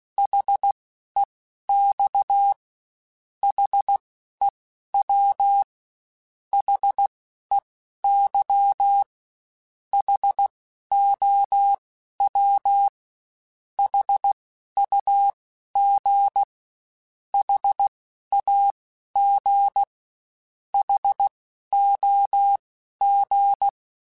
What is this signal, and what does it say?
Morse code, heard as a keyed tone, about 16 wpm, spaced as for 12 wpm, 800 Hz. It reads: HEX HEW HEY HOW HUG HAG HOG
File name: H Prefixes / H mp3s